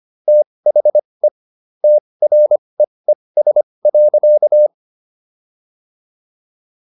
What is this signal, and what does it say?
Morse code, heard as a keyed tone, about 25 wpm, spaced as for 20 wpm, 600 Hz. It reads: THE TREES.